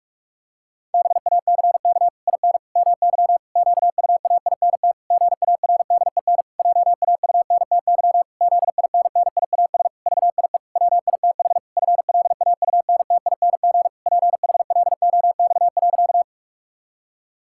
Morse code, heard as a keyed tone, about 46 wpm, 700 Hz. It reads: BACK IN MY QUAINT GARDEN JAUNTY ZINNIAS VIE WITH FLAUNTING PHLOX.
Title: Pangram Challenge v2 - 45wpm